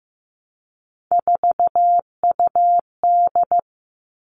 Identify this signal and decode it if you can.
Morse code, heard as a keyed tone, about 15 wpm, 700 Hz. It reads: 4UD
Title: All Cut Numbers - 3-Digit 15wpm